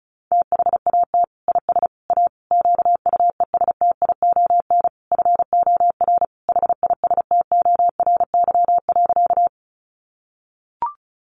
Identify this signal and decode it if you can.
Morse code, heard as a keyed tone, about 34 wpm, 700 Hz. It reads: THAT IS A QUESTION FOR HISTORY.
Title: Sentences from Top 600 Words 35wpm